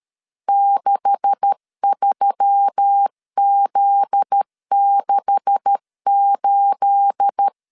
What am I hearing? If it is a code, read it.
Morse code, heard as a keyed tone, about 13 wpm, 800 Hz. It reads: 63Z68
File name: LEZIONI / 9 286 12wpm 2305